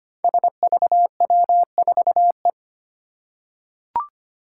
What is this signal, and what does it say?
Morse code, heard as a keyed tone, about 25 wpm, 700 Hz. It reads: SVW4E
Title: Five Challenging Letter-Number Combinations 25wpm